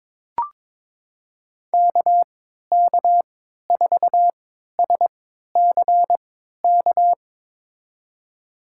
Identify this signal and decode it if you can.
Morse code, heard as a keyed tone, about 21 wpm, spaced as for 12 wpm, 700 Hz. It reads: KK4SCK
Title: USA Call Signs - 6 Characters - 3x Character Spacing - 22wpm